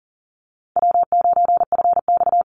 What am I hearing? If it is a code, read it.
Morse code, heard as a keyed tone, about 40 wpm, 700 Hz. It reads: W9FX
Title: Easy Call Signs Repeated 2x - 40wpm